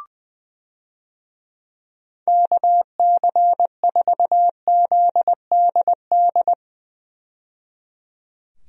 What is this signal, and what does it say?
Morse code, heard as a keyed tone, about 20 wpm, 700 Hz. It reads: KC4ZDD
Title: USA Call Signs - 6 Characters - 20wpm